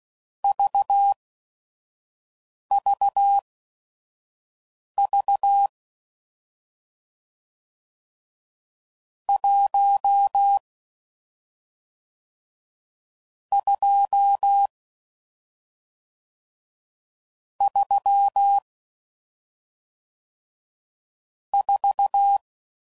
Morse code, heard as a keyed tone, about 16 wpm, spaced as for 5 wpm, 800 Hz. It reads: VVV 1 2 3 4